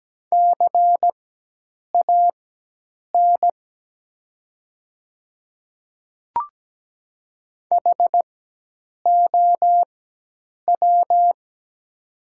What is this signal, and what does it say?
Morse code, heard as a keyed tone, about 17 wpm, spaced as for 8 wpm, 700 Hz. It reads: CAN HOW
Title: 4x Character Spacing - Top 100 Words - 17wpm